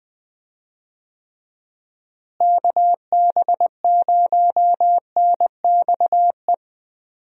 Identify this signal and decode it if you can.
Morse code, heard as a keyed tone, about 20 wpm, 700 Hz. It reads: KB0NXE
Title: USA Call Signs - 6 Characters - Repeated 3x - 20wpm